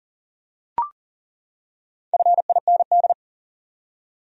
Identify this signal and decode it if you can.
Morse code, heard as a keyed tone, about 40 wpm, 700 Hz. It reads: FIND
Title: Top 100 Words 40wpm